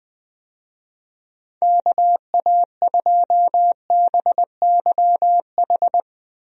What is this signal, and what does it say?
Morse code, heard as a keyed tone, about 19 wpm, 700 Hz. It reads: KA2BYH
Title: USA Call Signs - 6 Characters - Repeated 2x - 20wpm